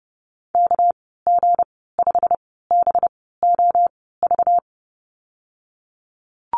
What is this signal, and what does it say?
Morse code, heard as a keyed tone, about 30 wpm, spaced as for 17 wpm, 700 Hz. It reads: KG5BOV
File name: Lesson 37g - Letter x - Call Signs 30wpm